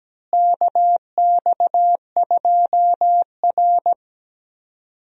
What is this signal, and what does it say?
Morse code, heard as a keyed tone, about 17 wpm, 700 Hz. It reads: KX2R
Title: USA Call Signs - 4 Characters - Repeated 3x - 17wpm